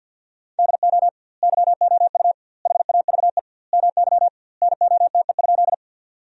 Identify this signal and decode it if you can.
Morse code, heard as a keyed tone, about 50 wpm, 700 Hz. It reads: DO YOU HAVE MY NOTE?
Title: 2x Word-Spacing - Sentences from Top 500 Words - 50wpm